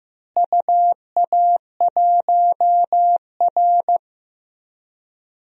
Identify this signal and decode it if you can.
Morse code, heard as a keyed tone, about 15 wpm, 700 Hz. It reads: UA1R
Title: DXCC-100 Call Signs - 4 Characters - Repeated 3x - 15wpm